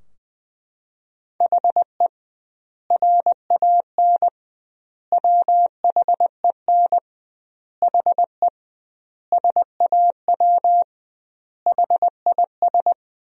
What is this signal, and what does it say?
Morse code, heard as a keyed tone, about 20 wpm, 700 Hz. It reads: HE RAN WHEN HE SAW HIS
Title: 2x Word-Spacing - Sentences from Top 500 Words - 20wpm